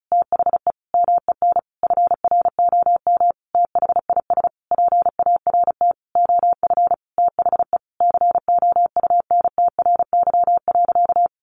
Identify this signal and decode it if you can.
Morse code, heard as a keyed tone, about 34 wpm, 700 Hz. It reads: THE MEN FROM THIS PART OF THE COUNTRY.
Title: Sentences from Top 300 Words 35wpm